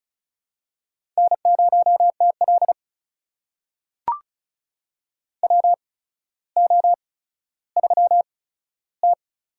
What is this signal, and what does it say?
Morse code, heard as a keyed tone, about 36 wpm, spaced as for 15 wpm, 700 Hz. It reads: N0TL WO3T
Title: USA Call Signs - 4 Characters - 8x Character Spacing - 35wpm